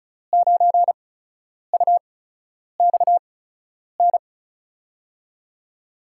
Morse code, heard as a keyed tone, about 36 wpm, spaced as for 10 wpm, 700 Hz. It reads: 9UXN
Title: USA Call Signs - 8x Character Spacing - 35wpm